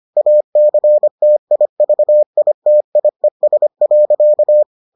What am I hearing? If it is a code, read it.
Morse code, heard as a keyed tone, about 25 wpm, 600 Hz. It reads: ACTIVITIES.